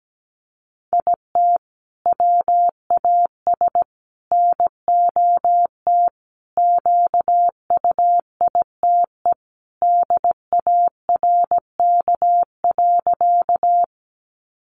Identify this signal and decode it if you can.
Morse code, heard as a keyed tone, about 17 wpm, 700 Hz. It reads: IT WAS NOT QUITE DARK.
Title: Sentences from Top 700 Words 17wpm